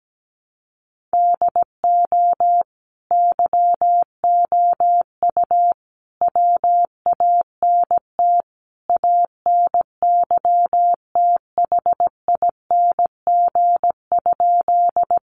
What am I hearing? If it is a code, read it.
Morse code, heard as a keyed tone, about 17 wpm, 700 Hz. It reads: DO YOU WANT ANYTHING?
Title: Sentences from Top 1000 Words 17wpm